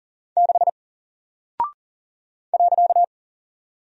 Morse code, heard as a keyed tone, about 40 wpm, 700 Hz. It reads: 6 .
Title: Single Character (Letter:Number:Punctuation:Prosign) Alt 40wpm